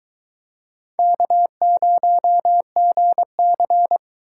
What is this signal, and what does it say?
Morse code, heard as a keyed tone, about 23 wpm, 700 Hz. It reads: K0GC